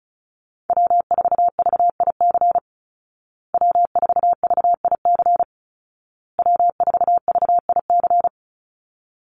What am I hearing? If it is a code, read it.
Morse code, heard as a keyed tone, about 34 wpm, 700 Hz. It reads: W4VIC W4VIC W4VIC
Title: Call Signs Repeated 2x - 35wpm